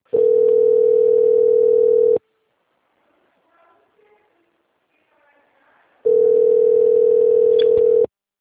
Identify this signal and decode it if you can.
Ringback tone (North America), 440 + 480 Hz.